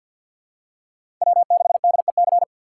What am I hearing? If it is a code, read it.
Morse code, heard as a keyed tone, about 54 wpm, 700 Hz. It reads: W6DEC
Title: USA Call Signs - 5 Characters - 8x Character Spacing - 50wpm